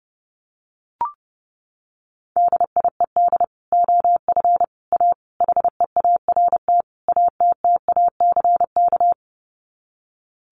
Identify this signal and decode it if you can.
Morse code, heard as a keyed tone, about 30 wpm, 700 Hz. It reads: DIED OF A HEART ATTACK
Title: Sets of 5 Words Encore2 30wpm